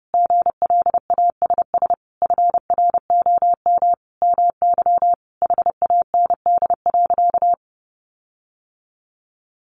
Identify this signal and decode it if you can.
Morse code, heard as a keyed tone, about 28 wpm, 700 Hz. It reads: GLASS FROM MY HAND.
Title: Sentences from Top 1300 Words 30wpm